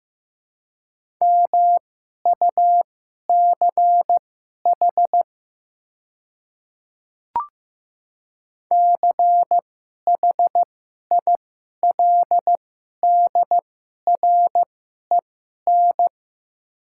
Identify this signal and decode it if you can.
Morse code, heard as a keyed tone, about 15 wpm, spaced as for 11 wpm, 700 Hz. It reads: MUCH CHILDREN
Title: 2x Character Spacing - Top 300 Words - 15wpm